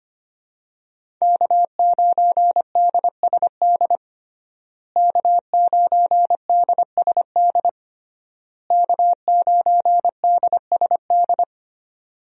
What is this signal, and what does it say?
Morse code, heard as a keyed tone, about 25 wpm, 700 Hz. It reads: K9DSD K9DSD K9DSD
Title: USA Call Signs - 5 Characters - Repeated 3x - 25wpm